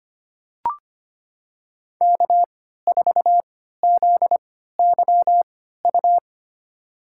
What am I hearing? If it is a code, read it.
Morse code, heard as a keyed tone, about 25 wpm, spaced as for 14 wpm, 700 Hz. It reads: K4ZYU
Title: USA Call Signs - 5 Characters - 3x Character Spacing - 25wpm